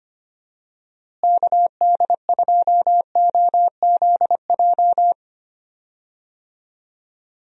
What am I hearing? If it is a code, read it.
Morse code, heard as a keyed tone, about 25 wpm, 700 Hz. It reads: KD2OZJ
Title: USA Call Signs - 6 Characters - Repeated 2x - 25wpm